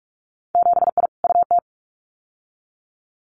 Morse code, heard as a keyed tone, about 46 wpm, 700 Hz. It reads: 7I UT
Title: Winter Field Day Exchanges Encore 45wpm